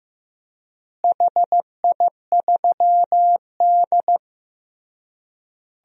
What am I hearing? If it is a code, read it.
Morse code, heard as a keyed tone, about 15 wpm, 700 Hz. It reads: HI3D
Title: DXCC-100 Call Signs - 4 Characters - Repeated 3x - 15wpm